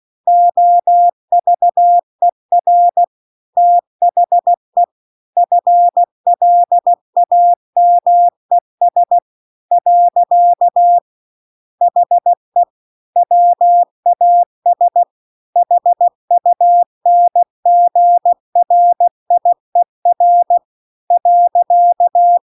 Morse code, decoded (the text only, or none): OVER THE FLAMES . HE WAS HUNGRIER .